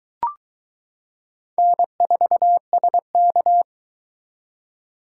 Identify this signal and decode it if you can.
Morse code, heard as a keyed tone, about 21 wpm, 700 Hz. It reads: N4SK